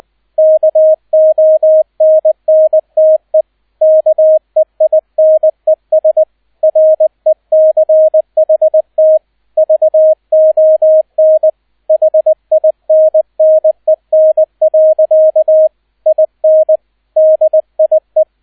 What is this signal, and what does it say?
Morse code, heard as a keyed tone, about 20 wpm, 630 Hz, with radio noise. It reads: KONNTE KEINES RECHT VON HINNEN. IN DIE
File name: Erster Streich 20wpm